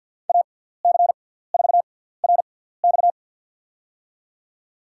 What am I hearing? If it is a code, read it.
Morse code, heard as a keyed tone, about 49 wpm, spaced as for 17 wpm, 700 Hz. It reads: AC4RX
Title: USA Call Signs - 5 Characters - 6x Character Spacing - 50wpm